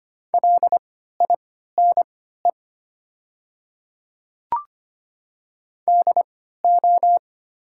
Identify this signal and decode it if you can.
Morse code, heard as a keyed tone, about 25 wpm, spaced as for 14 wpm, 700 Hz. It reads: LINE DO
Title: 3x Character Spacing - Top 300 Words - 25wpm